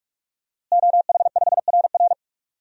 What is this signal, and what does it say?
Morse code, heard as a keyed tone, about 47 wpm, 700 Hz. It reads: OH5RR
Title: DXCC-100 Call Signs - 5 Characters - Repeated 3x - 45wpm